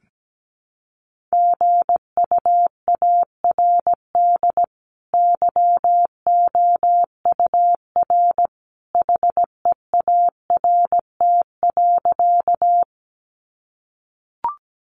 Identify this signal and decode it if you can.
Morse code, heard as a keyed tone, about 17 wpm, 700 Hz. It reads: GUARD YOUR HEART.
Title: Sentences from Top 1200 Words 17wpm